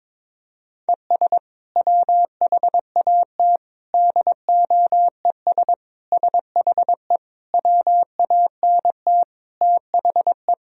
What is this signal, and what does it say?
Morse code, heard as a keyed tone, about 21 wpm, 700 Hz. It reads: ES WHAT DOES SHE WANT THE
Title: Sentences from Top 400 Words and Abbreviations 22wpm